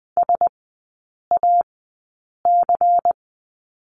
Morse code, heard as a keyed tone, about 20 wpm, spaced as for 8 wpm, 690 Hz. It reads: SAC
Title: 093 Lesson 16e - Letter c - Three Characters 20wpm